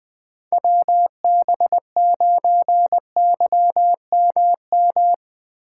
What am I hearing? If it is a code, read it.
Morse code, heard as a keyed tone, about 20 wpm, 700 Hz. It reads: WB9YMM
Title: USA Call Signs - 6 Characters - 2x Character Spacing - 20wpm